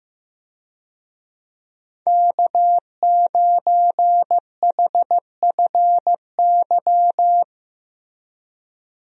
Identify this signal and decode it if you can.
Morse code, heard as a keyed tone, about 16 wpm, 700 Hz. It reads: K9HFY
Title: USA Call Signs - 5 Characters - Repeated 2x - 15wpm